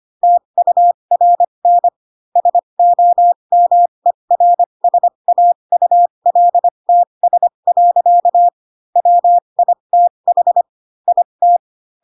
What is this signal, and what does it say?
Morse code, heard as a keyed tone, about 25 wpm, spaced as for 22 wpm, 700 Hz. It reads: TURN SOMERSAULTS. WITH IT
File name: Pinocchio / Pinocchio ch2 18 25 ik1hge